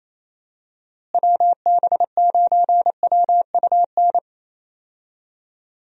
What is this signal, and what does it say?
Morse code, heard as a keyed tone, about 27 wpm, 700 Hz. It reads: WB9WUN
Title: USA Call Signs - 6 Characters - Repeated 2x - 28wpm